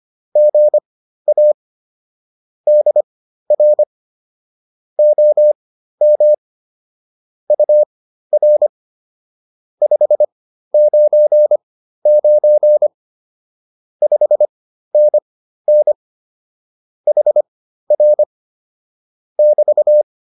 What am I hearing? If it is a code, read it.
Morse code, heard as a keyed tone, about 25 wpm, spaced as for 13 wpm, 600 Hz. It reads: GA DR OM UR 599 5NN HR <BT>